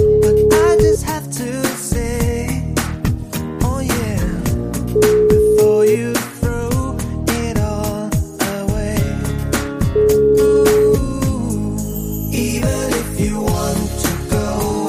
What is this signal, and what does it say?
Ringback tone (Europe, 425 Hz), with noise.